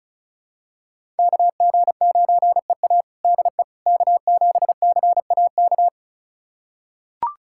Morse code, heard as a keyed tone, about 36 wpm, 700 Hz. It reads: KG9EA DE K7CAK
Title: USA Callsign DE Callsign - 5 Characters - 35wpm